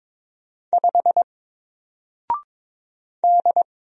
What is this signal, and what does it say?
Morse code, heard as a keyed tone, about 22 wpm, 700 Hz. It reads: 5 D
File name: Single-Letter-Number-22wpm